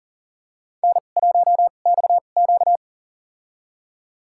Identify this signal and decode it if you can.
Morse code, heard as a keyed tone, about 40 wpm, spaced as for 29 wpm, 700 Hz. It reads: N1XQ
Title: USA Call Signs - 2x Character Spacing - 40wpm